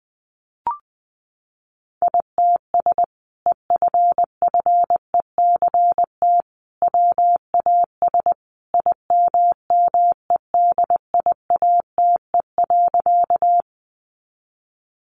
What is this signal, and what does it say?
Morse code, heard as a keyed tone, about 19 wpm, 700 Hz. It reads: ITS EFFECT WAS IMMEDIATE.
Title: Sentences from Top 1200 Words 20wpm